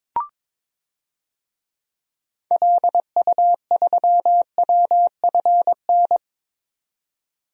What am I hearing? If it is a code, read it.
Morse code, heard as a keyed tone, about 22 wpm, 700 Hz. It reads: LU3WFN
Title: DXCC-100 Call Signs - 6 Characters - 22wpm